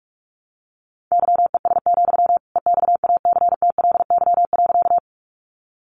Morse code, heard as a keyed tone, about 44 wpm, 700 Hz. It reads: YES, EXACTLY.
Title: Sentences from Top 900 Words 45wpm